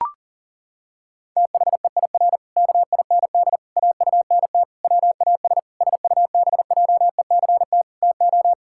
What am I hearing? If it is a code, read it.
Morse code, heard as a keyed tone, about 40 wpm, 700 Hz. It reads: THEIR KIND AUNT WAS SUBJECT TO
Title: Pangram Challenge v2 - 40wpm